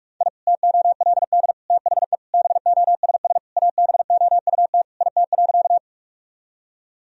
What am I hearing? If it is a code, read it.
Morse code, heard as a keyed tone, about 46 wpm, 700 Hz. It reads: I TOLD THE BOSS ABOUT IT.
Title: Sentences from Top 1000 Words - Review Part 4 - 45wpm